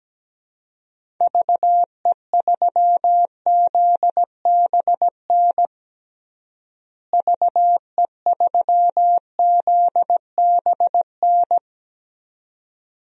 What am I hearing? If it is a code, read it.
Morse code, heard as a keyed tone, about 17 wpm, 700 Hz. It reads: VE3ZBN VE3ZBN
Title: DXCC-100 Call Signs - 6 Characters - Repeated 2x - 17wpm